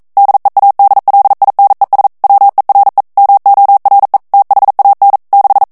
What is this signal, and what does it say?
Morse code, heard as a keyed tone, about 42 wpm, 800 Hz. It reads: DEADLINES WERE MORE THAN 6